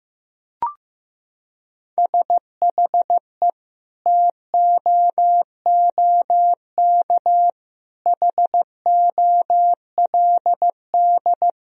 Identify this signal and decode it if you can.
Morse code, heard as a keyed tone, about 15 wpm, 700 Hz. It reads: SHE TOOK HOLD
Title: Sentences from Top 1000 Words - Review Part 2 - 15wpm